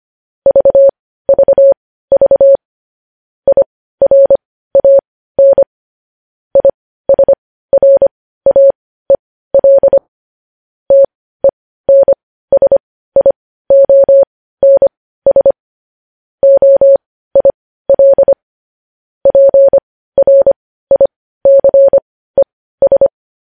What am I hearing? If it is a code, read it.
Morse code, heard as a keyed tone, about 25 wpm, spaced as for 15 wpm, 550 Hz. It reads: VVV IRAN ISRAEL TENSIONS OIL PRICES
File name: podcast / News headlines-1x-25x15